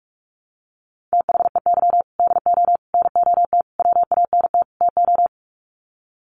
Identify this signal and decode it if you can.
Morse code, heard as a keyed tone, about 44 wpm, 700 Hz. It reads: THEY DO NOT WANT TO
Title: Sets of 5 Words 45wpm